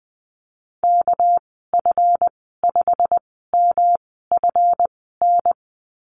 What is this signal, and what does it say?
Morse code, heard as a keyed tone, about 20 wpm, spaced as for 14 wpm, 700 Hz. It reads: KF5MFN
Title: 245 Lesson 39g - Letter z - Call Signs 20wpm